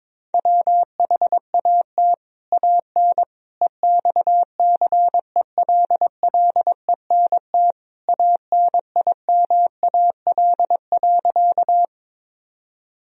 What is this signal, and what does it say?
Morse code, heard as a keyed tone, about 22 wpm, 700 Hz. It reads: WHAT AN EXCELLENT ANIMAL.
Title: Sentences from Top 1000 Words - Review Part 1 - 22wpm